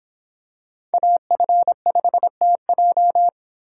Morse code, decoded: AF5TJ

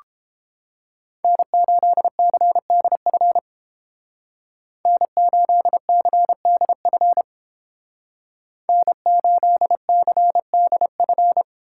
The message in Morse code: N8CDF N8CDF N8CDF

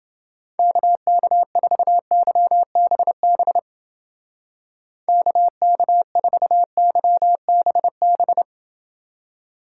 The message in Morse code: KK4YBB KK4YBB

